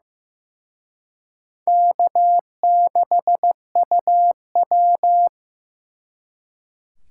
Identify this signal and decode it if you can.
Morse code, heard as a keyed tone, about 18 wpm, 700 Hz. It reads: K6UW
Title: Speed Racing - USA Call Signs 15wpm